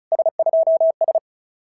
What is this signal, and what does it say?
Morse code, heard as a keyed tone, about 36 wpm, 690 Hz. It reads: S2S
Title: SOTA QSO Practice 35wpm